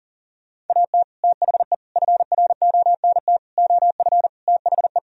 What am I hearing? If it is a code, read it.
Morse code, heard as a keyed tone, about 52 wpm, 700 Hz. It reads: AT THE FRONT OF THE